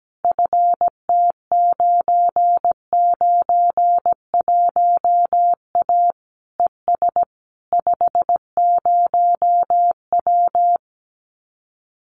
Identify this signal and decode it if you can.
Morse code, heard as a keyed tone, about 17 wpm, 700 Hz. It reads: FT991A ES 50W
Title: Common QSO Phrases Encore 17wpm